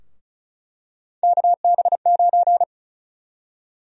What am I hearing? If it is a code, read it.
Morse code, heard as a keyed tone, about 36 wpm, 700 Hz. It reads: KB9
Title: USA Call Sign Prefixes - 35wpm